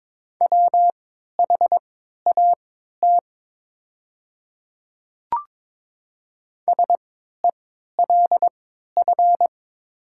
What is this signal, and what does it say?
Morse code, heard as a keyed tone, about 22 wpm, spaced as for 12 wpm, 700 Hz. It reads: WHAT SELF
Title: 3x Character Spacing - Top 300 Words - 22wpm